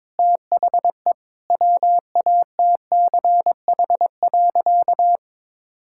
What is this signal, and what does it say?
Morse code, heard as a keyed tone, about 22 wpm, 700 Hz. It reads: THE WATCH.
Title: Sentences from Top 1600 Words 22wpm